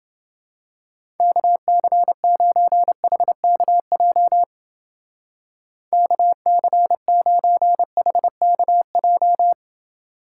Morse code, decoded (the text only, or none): KC9HKJ KC9HKJ